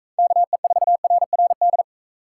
Morse code, heard as a keyed tone, about 38 wpm, 700 Hz. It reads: KE4RRD